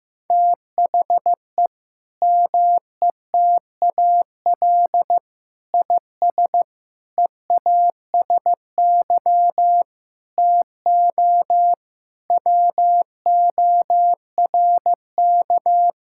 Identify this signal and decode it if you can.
Morse code, heard as a keyed tone, about 15 wpm, 700 Hz. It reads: THE METAL IS EASY TO WORK
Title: Sentences from Top 1000 Words - Review Part 6 - 15wpm